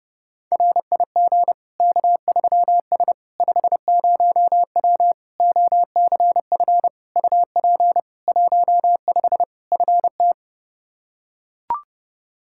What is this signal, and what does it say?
Morse code, heard as a keyed tone, about 32 wpm, 700 Hz. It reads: RIG K3S 50W OCF UP 15 FT
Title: Practice QSO - Exchange 2 - Info - 30wpm